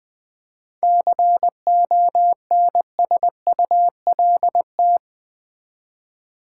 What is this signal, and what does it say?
Morse code, heard as a keyed tone, about 20 wpm, 700 Hz. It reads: CONSULT
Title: Seven Letter Words - 20wpm